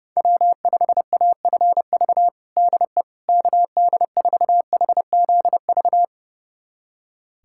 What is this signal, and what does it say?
Morse code, heard as a keyed tone, about 30 wpm, 700 Hz. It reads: W5AFV DE KD4HZV
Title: USA Callsign DE Callsign 30wpm